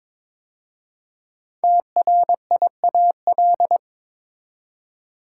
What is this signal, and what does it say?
Morse code, heard as a keyed tone, about 22 wpm, 700 Hz. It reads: TRIAL
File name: 5 Letter Words - First 3 Letters Given - 22wpm